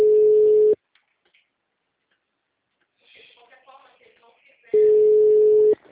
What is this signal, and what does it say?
Ringback tone (Europe, 425 Hz), with speech.
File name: out- -2005- - - .39032